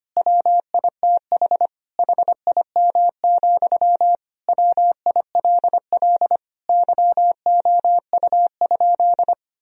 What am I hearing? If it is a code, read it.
Morse code, heard as a keyed tone, about 25 wpm, 700 Hz. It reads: WITH HIM, WILL YOU?
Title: Sentences from Top 1000 Words - Review Part 1 - 25wpm